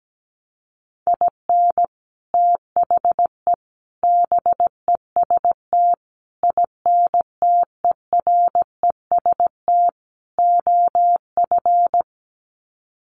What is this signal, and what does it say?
Morse code, heard as a keyed tone, about 17 wpm, 700 Hz. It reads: IN THE BEST INTEREST OF
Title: Sets of 5 Words Encore 17wpm